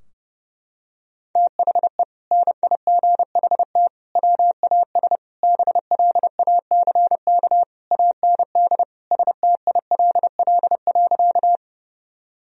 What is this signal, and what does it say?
Morse code, heard as a keyed tone, about 32 wpm, 700 Hz. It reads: THE NIGHT WAS BLACK AND STILL.
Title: Sentences from Top 1000 Words - Review Part 2 - 30wpm